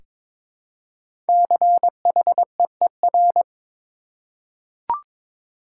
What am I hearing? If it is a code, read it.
Morse code, heard as a keyed tone, about 22 wpm, 700 Hz. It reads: CHEER
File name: 5 Letter Words - First 3 Letters Given - 22wpm